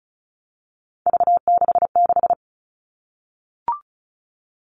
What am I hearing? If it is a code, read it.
Morse code, heard as a keyed tone, about 34 wpm, 700 Hz. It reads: V66
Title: All Cut Numbers - 3-Digit 35wpm